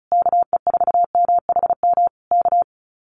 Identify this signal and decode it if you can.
Morse code, heard as a keyed tone, about 34 wpm, 700 Hz. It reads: KE4MHM K
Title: Common QSO Phrases Encore 35wpm